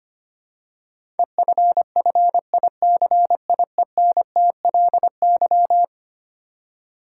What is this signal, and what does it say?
Morse code, heard as a keyed tone, about 25 wpm, 700 Hz. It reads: EFFICIENTLY
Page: morse-code-ninja-practice-25wpm